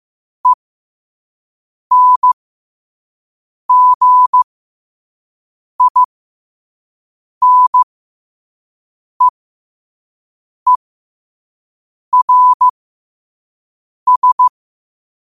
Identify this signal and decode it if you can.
Morse code, heard as a keyed tone, about 15 wpm, spaced as for 5 wpm, 1000 Hz. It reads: ENGINEERS
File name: qotdfiles / 5wpm